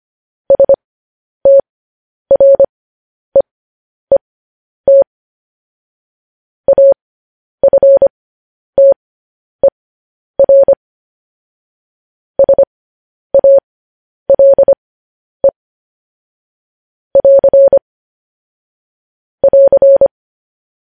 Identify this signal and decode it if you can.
Morse code, heard as a keyed tone, about 25 wpm, spaced as for 10 wpm, 550 Hz. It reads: STREET AFTER SALE <AR> <AR>
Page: news-headlines-in-morse-code-at-10-wpm